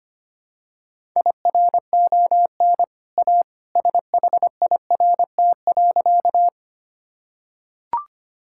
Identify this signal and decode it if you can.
Morse code, heard as a keyed tone, about 25 wpm, 700 Hz. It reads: IRON A SHIRT.
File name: Easy Sentences with Hints - 25wpm